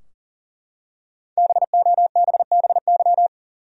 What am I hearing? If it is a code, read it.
Morse code, heard as a keyed tone, about 40 wpm, 700 Hz. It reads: BOBBY
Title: USA Common Names - Max 5 Letters - 40wpm